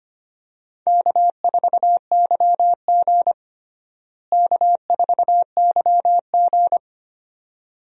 Morse code, heard as a keyed tone, about 25 wpm, 700 Hz. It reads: K4YG K4YG
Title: USA Call Signs - 4 Characters - Repeated 2x - 25wpm